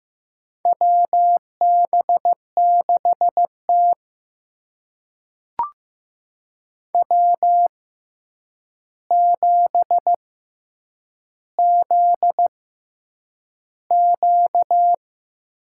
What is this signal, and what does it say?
Morse code, heard as a keyed tone, about 15 wpm, spaced as for 8 wpm, 700 Hz. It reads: WB6T W7ZQ
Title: USA Call Signs - 4 Characters - 6x Character Spacing - 15wpm